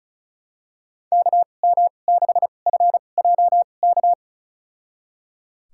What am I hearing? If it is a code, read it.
Morse code, heard as a keyed tone, about 36 wpm, spaced as for 25 wpm, 700 Hz. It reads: KM6FJK
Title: USA Call Signs - 2x Character Spacing - 35wpm